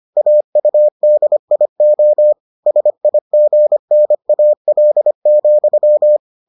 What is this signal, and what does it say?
Morse code, heard as a keyed tone, about 25 wpm, 600 Hz. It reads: AUDIO SIGNAL,